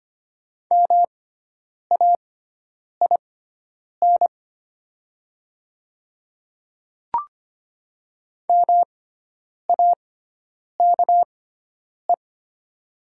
Morse code, heard as a keyed tone, about 25 wpm, spaced as for 9 wpm, 700 Hz. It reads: MAIN MAKE
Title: 6x Character Spacing - Top 500 Words - 25wpm